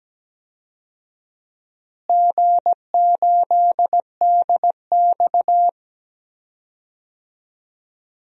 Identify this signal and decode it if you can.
Morse code, heard as a keyed tone, about 17 wpm, 700 Hz. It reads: G8DX
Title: DXCC-100 Call Signs - 4 Characters - Repeated 2x - 17wpm